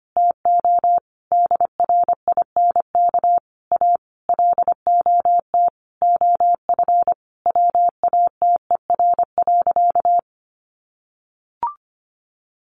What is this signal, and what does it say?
Morse code, heard as a keyed tone, about 25 wpm, 700 Hz. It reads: TO DRINK A LOT OF WATER.
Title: Additional Easy Phrases 25wpm